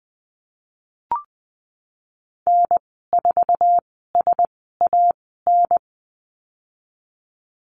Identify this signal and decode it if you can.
Morse code, heard as a keyed tone, about 20 wpm, spaced as for 15 wpm, 700 Hz. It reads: N4SAN
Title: 173 Lesson 28g - Number 9 - Call Signs 20wpm